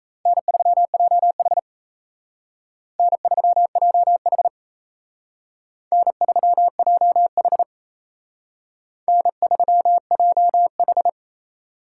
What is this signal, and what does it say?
Morse code, heard as a keyed tone, about 35 wpm, 700 Hz. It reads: N3JH N3JH N3JH N3JH
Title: Speed Racing - USA Call Signs 28wpm